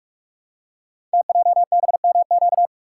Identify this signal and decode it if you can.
Morse code, heard as a keyed tone, about 44 wpm, 700 Hz. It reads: TJBMQ
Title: Code Groups - 5 Letters - 45wpm